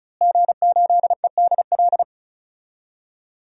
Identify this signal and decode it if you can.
Morse code, heard as a keyed tone, about 36 wpm, 700 Hz. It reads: G8EDL
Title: DXCC-100 Call Signs - 5 Characters - Repeated 2x - 35wpm